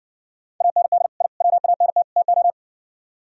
Morse code, heard as a keyed tone, about 60 wpm, 700 Hz. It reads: AND I WANT TO